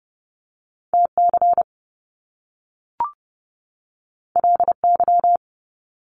Morse code, heard as a keyed tone, about 30 wpm, 700 Hz. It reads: TC LY
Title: Top Two Letter Combinations in Words 30wpm